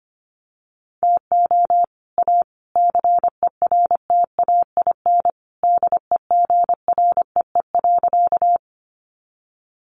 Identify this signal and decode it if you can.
Morse code, heard as a keyed tone, about 25 wpm, 700 Hz. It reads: TO A CERTAIN DEGREE.
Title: Sentences from Top 1500 Words 25wpm